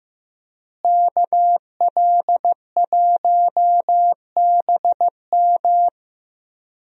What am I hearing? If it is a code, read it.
Morse code, heard as a keyed tone, about 15 wpm, 700 Hz. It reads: KL1BM
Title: USA Call Signs - 5 Characters - Repeated 3x - 15wpm